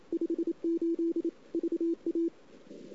Morse code, heard as a keyed tone, about 28 wpm, 330 Hz. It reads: 58 VA
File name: CW / 2008-ss-cw-k5tr-op-n5rz -01:00